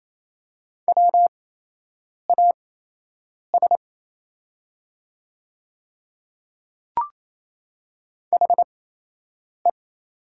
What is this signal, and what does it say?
Morse code, heard as a keyed tone, about 27 wpm, spaced as for 8 wpm, 700 Hz. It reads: WAS HE
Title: 8x Character Spacing - Top 200 Words - 28wpm